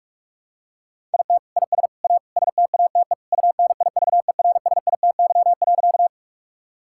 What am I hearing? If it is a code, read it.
Morse code, heard as a keyed tone, about 41 wpm, 700 Hz. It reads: IT IS A STATE UNIVERSITY.
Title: Sentences from Top 1000 Words - Review Part 5 - 45wpm